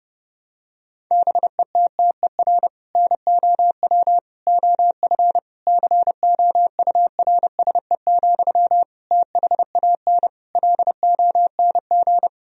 Morse code, heard as a keyed tone, about 32 wpm, 700 Hz. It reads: BETTER NOW OF COURSE, THAN LONG
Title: Sentences from Top 1000 Words - Review Part 3 - 30wpm